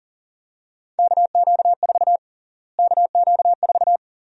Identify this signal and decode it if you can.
Morse code, heard as a keyed tone, about 40 wpm, 700 Hz. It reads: KQ4 KQ4
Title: USA Call Sign Prefixes - Repeated 2x - 40wpm